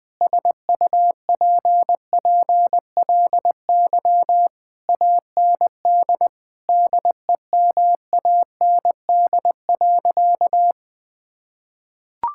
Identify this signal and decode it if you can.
Morse code, heard as a keyed tone, about 20 wpm, 700 Hz. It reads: SUPPLY AND DEMAND.
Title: Sentences from Top 1000 Words - Review Part 5 - 20wpm